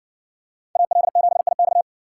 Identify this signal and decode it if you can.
Morse code, heard as a keyed tone, about 57 wpm, 700 Hz. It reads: AL7IX